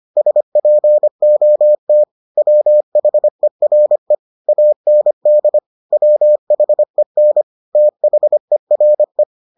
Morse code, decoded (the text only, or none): SPOT WHERE AND WHEN THERE